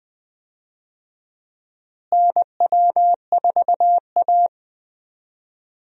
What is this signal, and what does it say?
Morse code, heard as a keyed tone, about 20 wpm, 690 Hz. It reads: NW4A
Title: USA Call Signs - 4 Characters - Repeated 3x - 20wpm